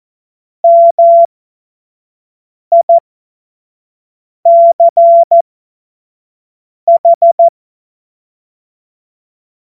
Morse code, heard as a keyed tone, about 14 wpm, spaced as for 5 wpm, 680 Hz. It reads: MICH
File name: Morsekurs / Witze / Witz 11. Mai 2021, 25 F70